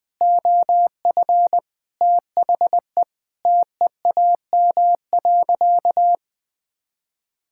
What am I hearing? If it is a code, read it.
Morse code, heard as a keyed tone, about 20 wpm, 700 Hz. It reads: OF THE TEAM.
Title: Sentences with 5-Word Phrases - 20wpm